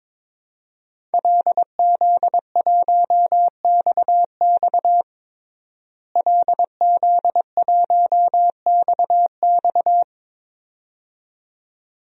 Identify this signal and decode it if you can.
Morse code, heard as a keyed tone, about 21 wpm, 700 Hz. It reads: LZ1XX LZ1XX
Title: DXCC-100 Call Signs - 5 Characters - Repeated 2x - 22wpm